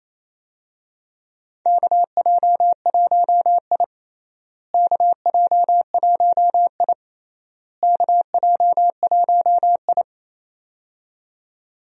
Morse code, heard as a keyed tone, about 28 wpm, 700 Hz. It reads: KJ1I KJ1I KJ1I
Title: USA Call Signs - 4 Characters - Repeated 3x - 28wpm